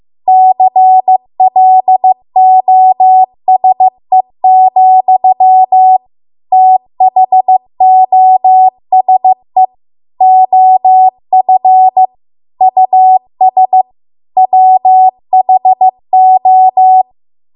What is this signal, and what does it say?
Morse code, heard as a keyed tone, about 15 wpm, 750 Hz. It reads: CLOSE, THOSE OF US WHO
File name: Archive / 15 WPM / 15WPM